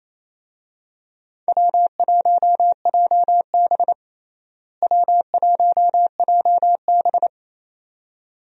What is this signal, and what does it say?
Morse code, heard as a keyed tone, about 28 wpm, 700 Hz. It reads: W1JB W1JB
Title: USA Call Signs - 4 Characters - Repeated 2x - 28wpm